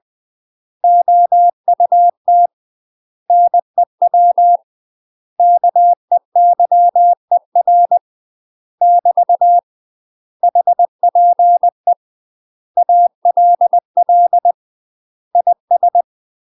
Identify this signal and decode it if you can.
Morse code, heard as a keyed tone, about 20 wpm, 700 Hz. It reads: OUT NEW KEYER <BT> HPE ALL IS
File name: UK MASTER QSOs TRAINING / 20WPM / UK MASTER QSOs 0044